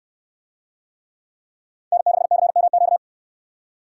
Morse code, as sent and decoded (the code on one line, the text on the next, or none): -. -.... -.-. -- --.-
N6CMQ